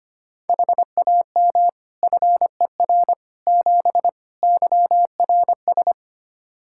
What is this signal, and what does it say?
Morse code, heard as a keyed tone, about 25 wpm, 700 Hz. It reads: HAM FER 7 YRS
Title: Practice QSO - Exchange 3 - More Info - 25wpm